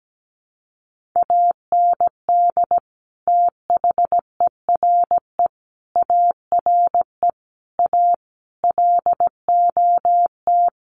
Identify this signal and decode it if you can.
Morse code, heard as a keyed tone, about 17 wpm, 700 Hz. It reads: AND THERE ARE A LOT
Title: Sets of 5 Words Encore2 17wpm